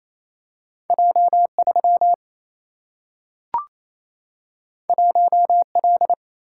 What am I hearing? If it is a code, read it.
Morse code, heard as a keyed tone, about 28 wpm, 700 Hz. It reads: J3 1L
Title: Single-Letter Single-Number 28wpm